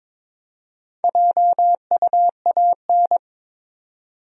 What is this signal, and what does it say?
Morse code, heard as a keyed tone, about 22 wpm, 700 Hz. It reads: JUAN
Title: USA Common Names - Max 4 Letters - 22wpm